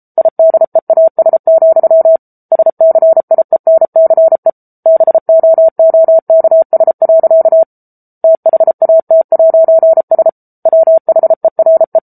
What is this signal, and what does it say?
Morse code, heard as a keyed tone, about 34 wpm, 650 Hz. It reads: IDEAS, SCIENCE BOOKS. THAT'S WHERE
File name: Book Two / 07-Chapter 7 - 33 WPM